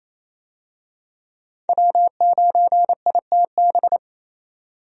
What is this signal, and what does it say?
Morse code, heard as a keyed tone, about 28 wpm, 700 Hz. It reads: W9ITB
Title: USA Call Signs - 5 Characters - 28wpm